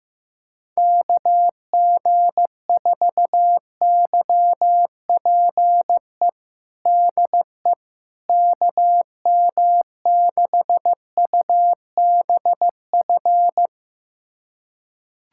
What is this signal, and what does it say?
Morse code, heard as a keyed tone, about 15 wpm, 700 Hz. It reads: KG4YPE DE KM6UBF
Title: USA Callsign DE Callsign 15wpm